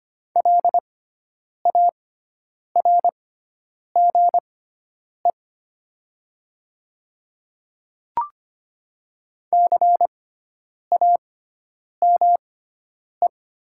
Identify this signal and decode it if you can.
Morse code, heard as a keyed tone, about 25 wpm, spaced as for 9 wpm, 700 Hz. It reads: LARGE CAME
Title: 6x Character Spacing - Top 300 Words - 25wpm